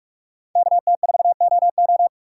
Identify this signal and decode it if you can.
Morse code, heard as a keyed tone, about 44 wpm, 690 Hz. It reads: KT4OO
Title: USA Call Signs - 5 Characters - 6x Character Spacing - 45wpm